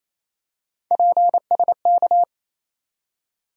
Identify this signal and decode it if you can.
Morse code, heard as a keyed tone, about 28 wpm, 700 Hz. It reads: PSK
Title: USA Call Sign Postfixes - 3 Letters 28wpm